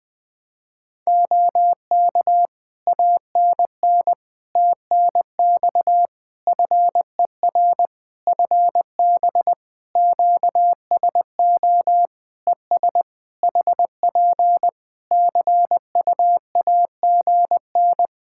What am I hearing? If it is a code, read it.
Morse code, heard as a keyed tone, about 20 wpm, 700 Hz. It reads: OK ANN TNX FER FB QSO ES HP CUAGN
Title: Practice QSO - Exchange 4 - Ending - 20wpm